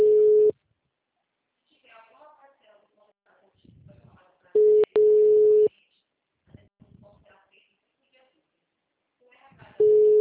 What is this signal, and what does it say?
Ringback tone (Europe, 425 Hz), with speech.